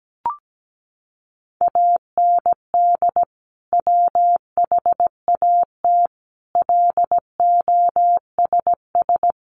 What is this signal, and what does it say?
Morse code, heard as a keyed tone, about 17 wpm, 700 Hz. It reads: AND WHAT LOSS
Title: Sentences from Top 1400 Words 17wpm